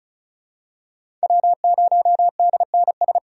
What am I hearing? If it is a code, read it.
Morse code, heard as a keyed tone, about 36 wpm, 700 Hz. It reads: W0DNS
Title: USA Call Signs - 5 Characters - 35wpm